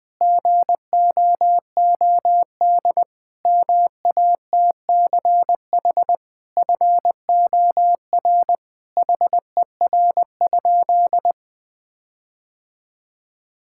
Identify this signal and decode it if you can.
Morse code, heard as a keyed tone, about 20 wpm, 700 Hz. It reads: GOOD MATCH FOR HER?
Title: Sentences from Top 1600 Words 20wpm